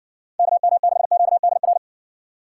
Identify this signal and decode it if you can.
Morse code, heard as a keyed tone, about 70 wpm, 700 Hz. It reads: KM6QDR